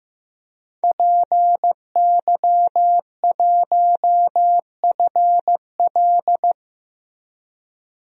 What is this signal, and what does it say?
Morse code, heard as a keyed tone, about 15 wpm, 700 Hz. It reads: PY1FL
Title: DXCC-100 Call Signs - 5 Characters - Repeated 2x - 15wpm